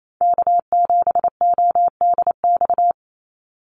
Morse code, heard as a keyed tone, about 28 wpm, 700 Hz. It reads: K7ODX
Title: NA SOTA Activator Callsigns 28wpm